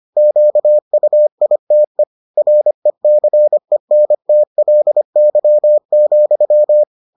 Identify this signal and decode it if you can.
Morse code, heard as a keyed tone, about 25 wpm, 600 Hz. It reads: QUITE RECENTLY,